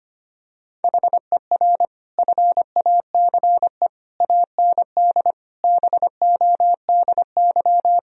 Morse code, heard as a keyed tone, about 25 wpm, 700 Hz. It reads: HER FACE AND BODY
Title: Sentences from Top 1000 Words - Review Part 2 - 25wpm